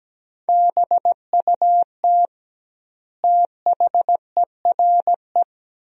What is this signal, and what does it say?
Morse code, heard as a keyed tone, about 17 wpm, 700 Hz. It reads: BUT THERE
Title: 2x Word-Spacing - Sentences from Top 100 Words - 17wpm